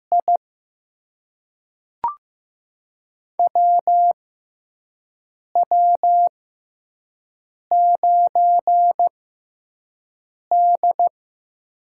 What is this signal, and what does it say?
Morse code, heard as a keyed tone, about 16 wpm, spaced as for 5 wpm, 700 Hz. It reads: I WW9D
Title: USA Call Signs - 4 Characters - 6x Character Spacing - 15wpm